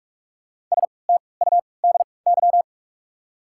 Morse code, heard as a keyed tone, about 42 wpm, spaced as for 25 wpm, 700 Hz. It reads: STUDY
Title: 3x Character Spacing - Top 300 Words - 45wpm